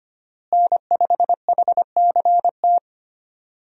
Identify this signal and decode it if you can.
Morse code, heard as a keyed tone, about 25 wpm, 700 Hz. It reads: N5HCT
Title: USA Call Signs - 5 Characters - Repeated 3x - 25wpm